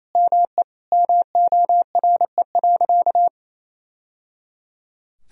Morse code, heard as a keyed tone, about 28 wpm, 700 Hz. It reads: ME MORE.
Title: Sentences from Top 1000 Words - Review Part 1 - 28wpm